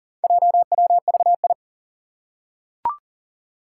Morse code, heard as a keyed tone, about 40 wpm, 700 Hz. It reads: JWVI
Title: Four Challenging Letter-Number Combinations 40wpm